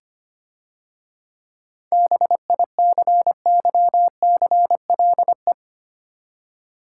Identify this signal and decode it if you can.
Morse code, heard as a keyed tone, about 25 wpm, 700 Hz. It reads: BICYCLE
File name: Easy Words - Speed-Racing with Hints - 25wpm